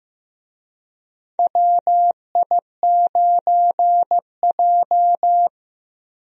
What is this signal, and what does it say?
Morse code, heard as a keyed tone, about 15 wpm, 700 Hz. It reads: WI9J
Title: USA Call Signs - 4 Characters - Repeated 3x - 15wpm